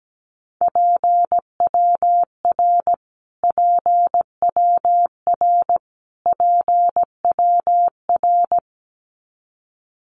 Morse code, heard as a keyed tone, about 17 wpm, 700 Hz. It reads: PWR PWR PWR
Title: Abbreviations Repeated 3x - 17wpm